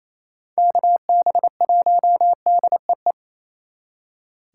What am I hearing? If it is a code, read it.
Morse code, heard as a keyed tone, about 33 wpm, 700 Hz. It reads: KB1DEE
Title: Speed Racing - USA 2x3 Call Signs 28wpm